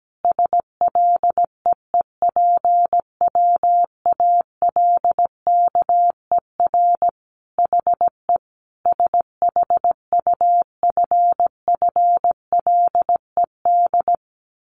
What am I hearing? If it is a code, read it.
Morse code, heard as a keyed tone, about 17 wpm, 700 Hz. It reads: SLEEPWALKER HE SHUFFLED
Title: A Sound Of Thunder 17wpm